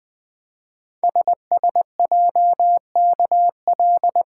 Morse code, heard as a keyed tone, about 20 wpm, 700 Hz. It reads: SSJKL
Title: Code Groups - 5 Letters - 20wpm